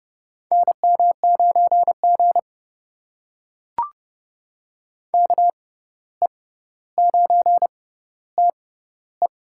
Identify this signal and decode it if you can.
Morse code, heard as a keyed tone, about 31 wpm, spaced as for 15 wpm, 700 Hz. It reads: NM9G KE9TE
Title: USA Call Signs - 6x Character Spacing - 30wpm